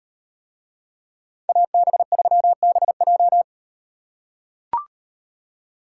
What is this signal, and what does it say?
Morse code, heard as a keyed tone, about 38 wpm, 700 Hz. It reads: AB3BJ